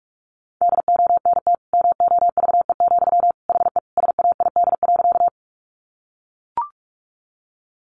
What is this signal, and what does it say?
Morse code, heard as a keyed tone, about 44 wpm, 700 Hz. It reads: DONT MOVE, HE SAID.
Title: Sentences from Top 300 Words 45wpm